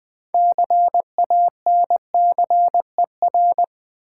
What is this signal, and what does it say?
Morse code, heard as a keyed tone, about 20 wpm, 700 Hz. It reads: CANCER
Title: 6 Letter Words - First 3 Letters Given - 20wpm